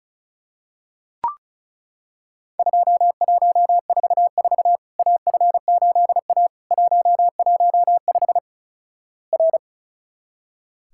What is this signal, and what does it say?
Morse code, heard as a keyed tone, about 36 wpm, 700 Hz. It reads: 2144 AF8A 115 R
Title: Worked All Europe DX Contest - QTC Exchange - 35wpm